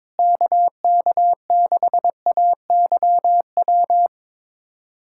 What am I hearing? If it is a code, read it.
Morse code, heard as a keyed tone, about 22 wpm, 700 Hz. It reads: KK6AYW
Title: USA Callsign DE Callsign - 6 Characters - 22wpm